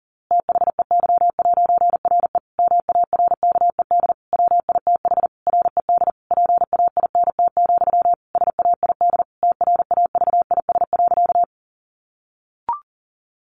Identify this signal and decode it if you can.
Morse code, heard as a keyed tone, about 40 wpm, 700 Hz. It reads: THEY'RE MARKED WITH RED PAINT, SAID TRAVIS.
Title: A Sound Of Thunder 40wpm